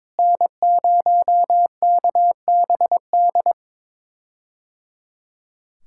Morse code, heard as a keyed tone, about 22 wpm, 700 Hz. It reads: N0KBD
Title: K1USN SST QSO Practice - 22wpm